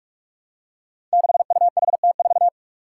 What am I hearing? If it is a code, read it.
Morse code, heard as a keyed tone, about 46 wpm, 700 Hz. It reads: 6UHT4
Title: Five Challenging Letter-Number Combinations 45wpm